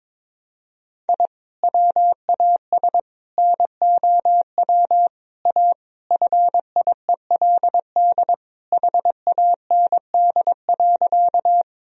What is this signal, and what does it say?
Morse code, heard as a keyed tone, about 22 wpm, 700 Hz. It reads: I WAS NOW A FIELD HAND.
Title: Sentences from Top 1000 Words - Review Part 2 - 22wpm